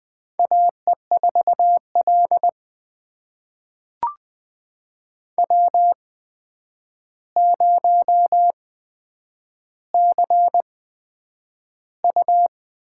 Morse code, heard as a keyed tone, about 20 wpm, spaced as for 9 wpm, 700 Hz. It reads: AE4L W0CU
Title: USA Call Signs - 4 Characters - 8x Character Spacing - 20wpm